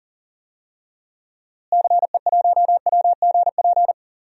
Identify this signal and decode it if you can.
Morse code, heard as a keyed tone, about 40 wpm, 700 Hz. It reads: CE1WGP
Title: DXCC-100 Call Signs - 6 Characters - Repeated 3x - 40wpm